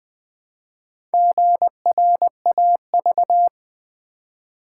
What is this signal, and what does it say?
Morse code, heard as a keyed tone, about 20 wpm, 700 Hz. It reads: GRAV
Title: Build A Word - 1 Letter At A Time - 5-Letter Words - 20wpm